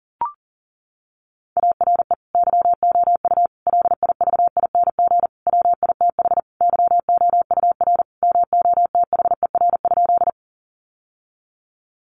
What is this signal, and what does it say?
Morse code, heard as a keyed tone, about 40 wpm, 700 Hz. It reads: ARE YOU LIVING WITH YOUR MOTHER?
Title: Sentences from Top 1100 Words 40wpm